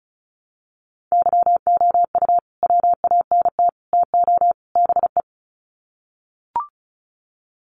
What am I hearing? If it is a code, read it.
Morse code, heard as a keyed tone, about 34 wpm, 700 Hz. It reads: YOU WANT TO BE
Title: Sets of 4 Words 35wpm